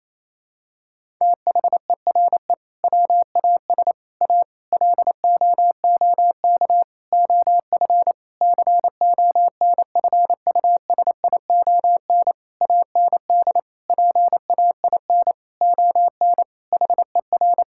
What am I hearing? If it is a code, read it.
Morse code, heard as a keyed tone, about 28 wpm, 700 Hz. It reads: THERE WAS A LOOK OF CONFUSION AND PAIN ON HER
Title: Sentences from Top 1600 Words 28wpm